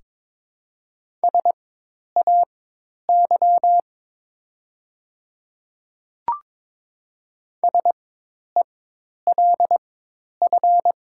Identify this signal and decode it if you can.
Morse code, heard as a keyed tone, about 22 wpm, spaced as for 10 wpm, 690 Hz. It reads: SAY SELF
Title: 4x Character Spacing - Top 500 Words - 22wpm